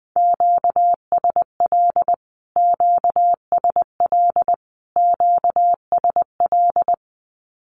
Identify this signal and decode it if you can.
Morse code, heard as a keyed tone, about 20 wpm, 700 Hz. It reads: QSL QSL QSL
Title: Q-Codes Repeated 3x 20wpm